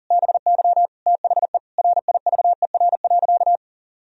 Morse code, heard as a keyed tone, about 40 wpm, 700 Hz. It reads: BY THE RIVER.
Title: Sentences from Top 1000 Words - Review Part 4 - 40wpm